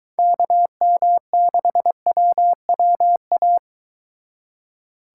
Morse code, heard as a keyed tone, about 23 wpm, 700 Hz. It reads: KM6WWA